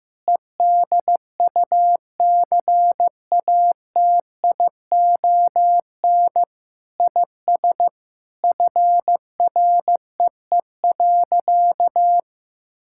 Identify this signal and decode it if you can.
Morse code, heard as a keyed tone, about 15 wpm, 700 Hz. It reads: EDUCATION IS FREE.
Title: Sentences from Top 1000 Words - Review Part 6 - 15wpm